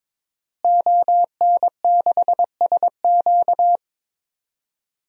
Morse code, heard as a keyed tone, about 22 wpm, 700 Hz. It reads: ON6SQ
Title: DXCC-100 Call Signs - 5 Characters - Repeated 3x - 22wpm